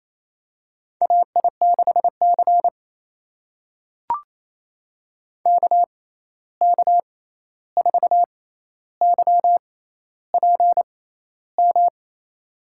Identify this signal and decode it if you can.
Morse code, heard as a keyed tone, about 28 wpm, spaced as for 13 wpm, 700 Hz. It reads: AI6C KK4YPM
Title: USA Call Signs - 6x Character Spacing - 28wpm